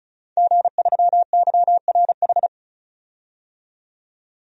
Morse code, heard as a keyed tone, about 34 wpm, 700 Hz. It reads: G3YRH
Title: DXCC-100 Call Signs - 5 Characters - 35wpm